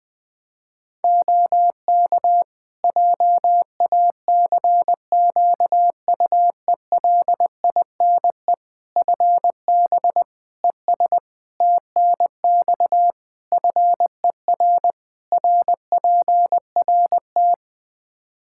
Morse code, heard as a keyed tone, about 20 wpm, 700 Hz. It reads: OK JACQUELINE FB ES TNX FER RPRT
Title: K1USN SST QSO Practice 20wpm